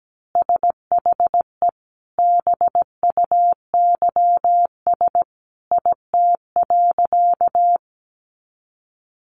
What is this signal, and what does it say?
Morse code, heard as a keyed tone, about 17 wpm, 700 Hz. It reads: SHE BUYS IT.
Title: Additional Easy Phrases 17wpm